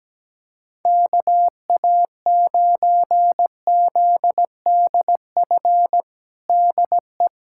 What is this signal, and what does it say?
Morse code, heard as a keyed tone, about 17 wpm, 700 Hz. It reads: KA9ZDF DE
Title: USA Callsign DE Callsign - 6 Characters - 17wpm